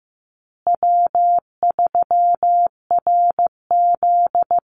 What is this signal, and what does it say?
Morse code, heard as a keyed tone, about 15 wpm, 700 Hz. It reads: W3RZ
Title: Easy Call Signs Repeated 3x - 15wpm